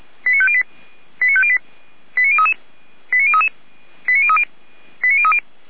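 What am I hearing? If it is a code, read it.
Five-tone selective call (ZVEI), heard as quick runs of five tones, tones of about 70 ms. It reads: 89508 89508 89033 89033 89030 89030
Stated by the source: ZVEI five-tone selective call 89099 received by SDR, from the Feuerwehr Orlamünde audio archive.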